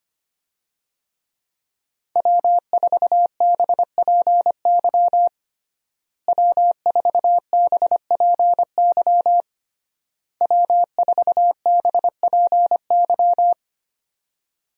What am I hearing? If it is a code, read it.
Morse code, heard as a keyed tone, about 25 wpm, 700 Hz. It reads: W4BPY W4BPY W4BPY
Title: USA Call Signs - 5 Characters - Repeated 3x - 25wpm